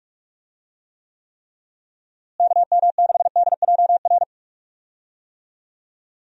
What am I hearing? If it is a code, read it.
Morse code, heard as a keyed tone, about 48 wpm, 700 Hz. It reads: KM6DJR
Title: USA Call Signs - 6 Characters - Repeated 3x - 45wpm